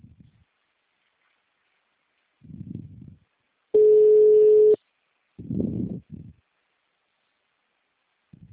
Ringback tone (Europe, 425 Hz).
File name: out- -2011- - - .259